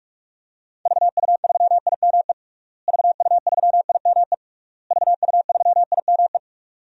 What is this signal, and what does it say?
Morse code, heard as a keyed tone, about 46 wpm, 700 Hz. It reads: VU3IME VU3IME VU3IME
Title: DXCC-100 Call Signs - 6 Characters - Repeated 3x - 45wpm